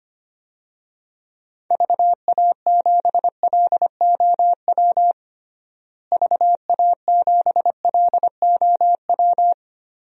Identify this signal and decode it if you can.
Morse code, heard as a keyed tone, about 25 wpm, 700 Hz. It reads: VA7LOW VA7LOW
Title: DXCC-100 Call Signs - 6 Characters - Repeated 3x - 25wpm